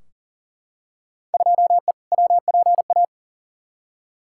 Morse code, heard as a keyed tone, about 40 wpm, 700 Hz. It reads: 2E WPA
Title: Field Day Exchanges - 40wpm